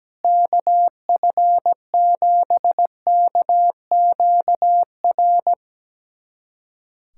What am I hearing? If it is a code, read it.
Morse code, heard as a keyed tone, about 17 wpm, 700 Hz. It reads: KF7KQR
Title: USA Callsign DE Callsign 17wpm